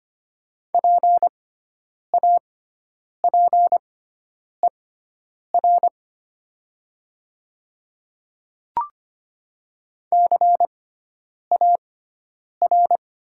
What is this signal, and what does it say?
Morse code, heard as a keyed tone, about 25 wpm, spaced as for 9 wpm, 700 Hz. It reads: PAPER CAR
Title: 6x Character Spacing - Top 300 Words - 25wpm